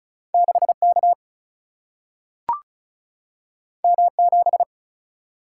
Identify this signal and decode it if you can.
Morse code, heard as a keyed tone, about 36 wpm, 700 Hz. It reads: 6 K M 7
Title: Single-Letter Single-Number 35wpm